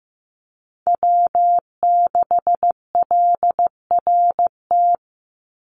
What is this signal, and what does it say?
Morse code, heard as a keyed tone, about 15 wpm, 700 Hz. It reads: W6LRT
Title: USA Call Signs 15wpm